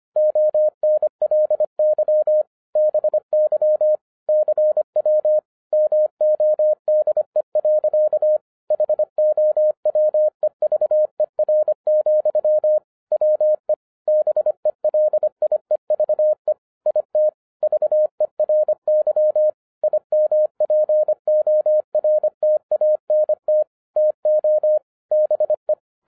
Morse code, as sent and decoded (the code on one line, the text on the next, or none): --- -. .-.. -.-- / -... -.-- / -.-. .-- / -- --- -.. .-.-.- / .... --- .-- . ...- . .-. --..-- / .-- . / -... .-.. .. ...- . / .. - / ...- . .-. -.-- / .. -- .--. --- .-. - .- -. - / - --- / -... .
ONLY BY CW MODE. HOWEVER, WE BELIEVE IT VERY IMPORTANT TO BE